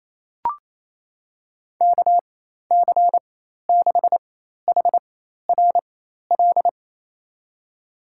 Morse code, heard as a keyed tone, about 28 wpm, spaced as for 13 wpm, 700 Hz. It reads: KC6HRL
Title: USA Call Signs - 4x Character Spacing - 28wpm